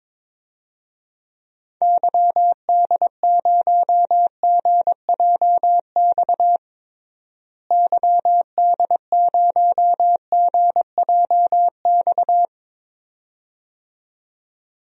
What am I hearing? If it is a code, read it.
Morse code, heard as a keyed tone, about 22 wpm, 700 Hz. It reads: YD0GJX YD0GJX
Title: DXCC-100 Call Signs - 6 Characters - Repeated 2x - 22wpm